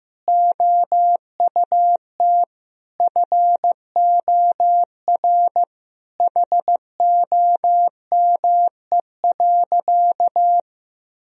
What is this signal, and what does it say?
Morse code, heard as a keyed tone, about 15 wpm, 700 Hz. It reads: OUT FOR HOME.
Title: Sentences from Top 1000 Words - Review Part 1 - 15wpm